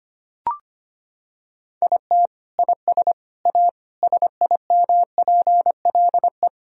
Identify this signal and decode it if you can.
Morse code, heard as a keyed tone, about 25 wpm, 700 Hz. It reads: IT IS A SIMPLE
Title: Sentences from Top 1000 Words - Review Part 2 - 25wpm